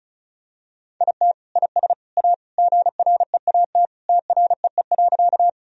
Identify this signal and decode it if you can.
Morse code, heard as a keyed tone, about 36 wpm, 700 Hz. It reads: IT IS A GREAT TREE.
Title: Sentences from Top 1000 Words - Review Part 1 - 35wpm